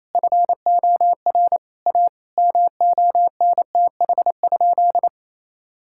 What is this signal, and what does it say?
Morse code, heard as a keyed tone, about 28 wpm, 700 Hz. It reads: FOR A MONTH?
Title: Sentences from Top 1000 Words - Review Part 5 - 28wpm